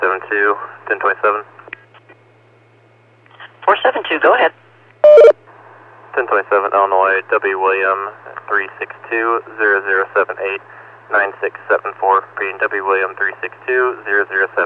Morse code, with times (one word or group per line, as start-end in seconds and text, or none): none